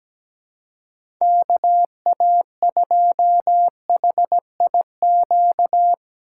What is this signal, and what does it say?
Morse code, heard as a keyed tone, about 17 wpm, 700 Hz. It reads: KA2HIQ
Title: USA Call Signs - 2x Character Spacing - 17wpm